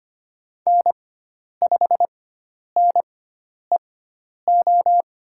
Morse code, heard as a keyed tone, about 25 wpm, spaced as for 10 wpm, 700 Hz. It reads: N5NEO
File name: Lesson 16g - Letter c - Call Signs 25wpm